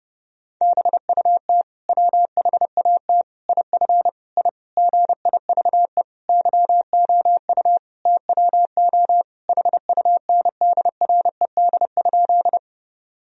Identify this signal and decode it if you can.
Morse code, heard as a keyed tone, about 30 wpm, 700 Hz. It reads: BUT WHAT IF I GIVE YOU TWO HUNDRED?
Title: Sentences from Top 1000 Words - Review Part 2 - 30wpm